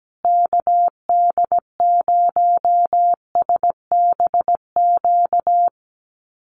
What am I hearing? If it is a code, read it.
Morse code, heard as a keyed tone, about 17 wpm, 700 Hz. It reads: KD0SBQ
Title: USA Call Signs Repeated 3x 17wpm